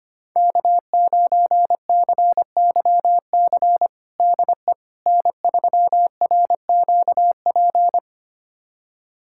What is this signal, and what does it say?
Morse code, heard as a keyed tone, about 25 wpm, 700 Hz. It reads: K9CYC DE N3RQP
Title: USA Callsign DE Callsign - 5 Characters - 25wpm